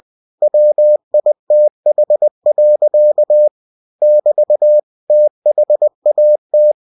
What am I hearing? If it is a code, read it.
Morse code, heard as a keyed tone, about 20 wpm, 600 Hz. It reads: WITH. <BT> THAT